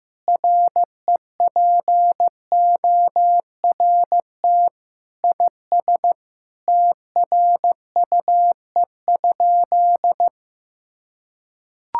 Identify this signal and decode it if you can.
Morse code, heard as a keyed tone, about 15 wpm, 700 Hz. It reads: REPORT IS TRUE?
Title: Sentences from Top 1000 Words - Review Part 6 - 15wpm